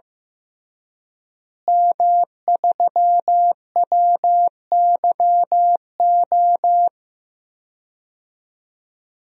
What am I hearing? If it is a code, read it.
Morse code, heard as a keyed tone, about 15 wpm, 700 Hz. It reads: M3WYO
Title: DXCC-100 Call Signs - 5 Characters - Repeated 3x - 15wpm